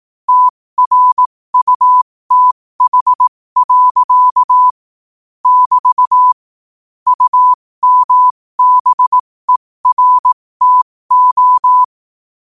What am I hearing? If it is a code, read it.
Morse code, heard as a keyed tone, about 18 wpm, spaced as for 15 wpm, 1000 Hz. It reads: TRUTH. <BT> UMBERTO
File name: qotdfiles / 15wpm